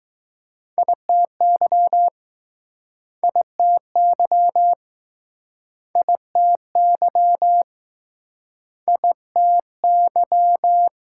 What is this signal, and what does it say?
Morse code, heard as a keyed tone, about 21 wpm, 700 Hz. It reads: ITY ITY ITY ITY